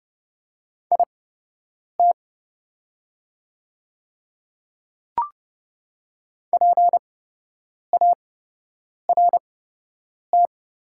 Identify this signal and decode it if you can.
Morse code, heard as a keyed tone, about 30 wpm, spaced as for 8 wpm, 700 Hz. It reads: IT PART
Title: 8x Character Spacing - Top 200 Words - 30wpm